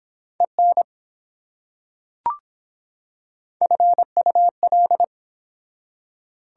Morse code, heard as a keyed tone, about 24 wpm, 700 Hz. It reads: EN FUL